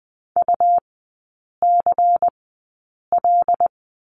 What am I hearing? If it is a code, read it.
Morse code, heard as a keyed tone, about 20 wpm, spaced as for 8 wpm, 700 Hz. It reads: UCL
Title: 133 Lesson 22e - Question Mark - Three Characters 20wpm